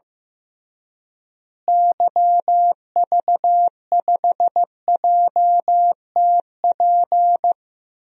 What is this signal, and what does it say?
Morse code, heard as a keyed tone, about 16 wpm, 700 Hz. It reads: YV5JTP
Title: DXCC-100 Call Signs - 6 Characters - Repeated 3x - 15wpm